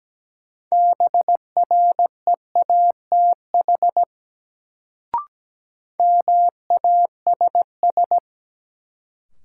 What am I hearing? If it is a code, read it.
Morse code, heard as a keyed tone, about 17 wpm, 700 Hz. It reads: BREATH MASS
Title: Sets of 5 Words Encore - New Word Drill - 17wpm